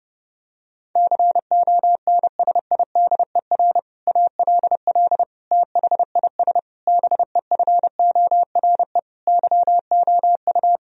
Morse code, heard as a keyed tone, about 30 wpm, 700 Hz. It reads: CONSIDER ALL THIS BEFORE YOU
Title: Sentences from Top 1000 Words - Review Part 4 - 30wpm